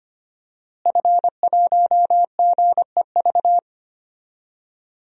Morse code, heard as a keyed tone, about 25 wpm, 700 Hz. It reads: F1GEV
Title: DXCC-100 Call Signs - 5 Characters - Repeated 3x - 25wpm